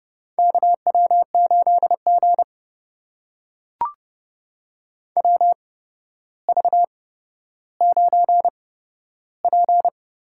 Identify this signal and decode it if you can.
Morse code, heard as a keyed tone, about 32 wpm, spaced as for 13 wpm, 700 Hz. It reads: KW8G WV9P
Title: USA Call Signs - 4 Characters - 8x Character Spacing - 30wpm